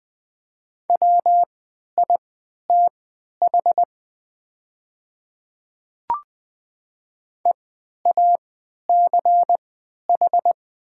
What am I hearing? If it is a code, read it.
Morse code, heard as a keyed tone, about 20 wpm, spaced as for 11 wpm, 700 Hz. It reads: WITH EACH
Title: 3x Character Spacing - Top 100 Words - 20wpm